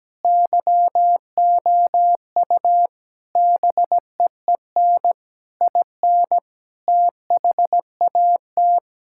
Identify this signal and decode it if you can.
Morse code, heard as a keyed tone, about 17 wpm, 700 Hz. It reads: YOU BEEN IN THAT
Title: Sentences from Top 1000 Words - Review Part 5 - 17wpm